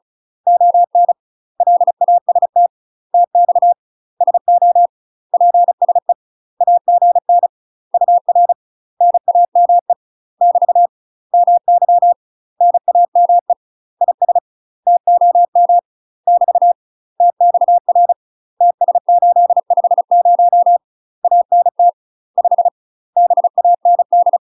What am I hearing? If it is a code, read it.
Morse code, heard as a keyed tone, about 34 wpm, 700 Hz. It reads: ON LAST TX SO PSE AGN UR NAME <BT> MY NAME IS TOM <BT> TXR TS850 ANT 5 BAND
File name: UK MASTER QSOs TRAINING / 35WPM / UK MASTER QSOs 0034